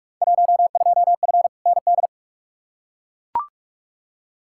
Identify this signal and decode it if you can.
Morse code, heard as a keyed tone, about 42 wpm, 700 Hz. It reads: 12F ND
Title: Field Day Exchanges Encore - 45wpm